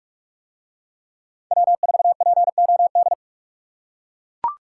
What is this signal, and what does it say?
Morse code, heard as a keyed tone, about 40 wpm, 700 Hz. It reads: W4POD